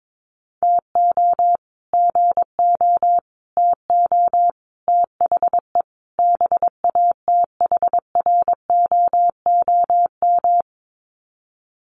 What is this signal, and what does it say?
Morse code, heard as a keyed tone, about 22 wpm, 700 Hz. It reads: TO GO TO THE BATHROOM
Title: Sets of 5 Words Encore 22wpm